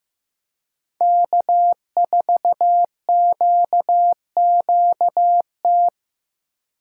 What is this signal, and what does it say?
Morse code, heard as a keyed tone, about 15 wpm, 700 Hz. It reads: K4QQT
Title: USA Call Signs - 5 Characters - 4x Character Spacing - 15wpm